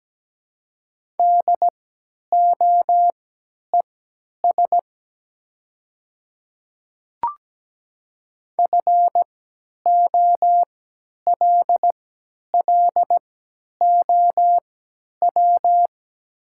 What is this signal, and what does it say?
Morse code, heard as a keyed tone, about 17 wpm, spaced as for 10 wpm, 700 Hz. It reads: DOES FOLLOW
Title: 3x Character Spacing - Top 200 Words - 17wpm